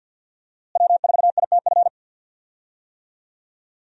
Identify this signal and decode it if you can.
Morse code, heard as a keyed tone, about 46 wpm, 700 Hz. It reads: W4ITF
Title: USA Call Signs - 5 Characters - Repeated 3x - 50wpm